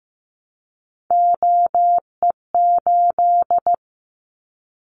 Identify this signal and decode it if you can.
Morse code, heard as a keyed tone, about 15 wpm, 700 Hz. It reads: OE8
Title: Callsign Prefixes for Top 100 DXCC - 15wpm